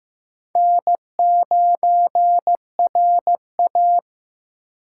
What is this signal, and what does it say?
Morse code, heard as a keyed tone, about 15 wpm, 700 Hz. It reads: N9RA
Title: USA Call Signs - 4 Characters - 8x Character Spacing - 15wpm